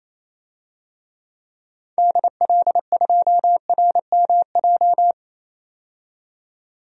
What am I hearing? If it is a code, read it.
Morse code, heard as a keyed tone, about 28 wpm, 700 Hz. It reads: DL2RMJ
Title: DXCC-100 Call Signs - 6 Characters - 28wpm